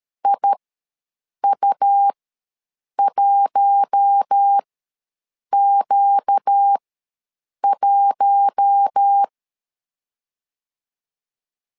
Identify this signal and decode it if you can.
Morse code, heard as a keyed tone, about 13 wpm, spaced as for 7 wpm, 800 Hz. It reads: IU1Q1
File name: LEZIONI / 4 Q1 ws 12wpm 2129